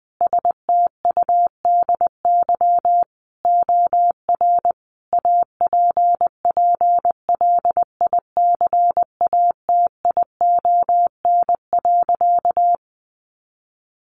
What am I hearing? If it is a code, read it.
Morse code, heard as a keyed tone, about 20 wpm, 700 Hz. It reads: STUDY OR APPLICATION.
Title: Sentences from Top 1200 Words 20wpm